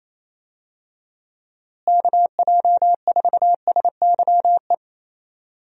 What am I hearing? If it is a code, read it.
Morse code, heard as a keyed tone, about 28 wpm, 700 Hz. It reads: KJ4SYE
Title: USA Call Signs - 6 Characters - Repeated 2x - 28wpm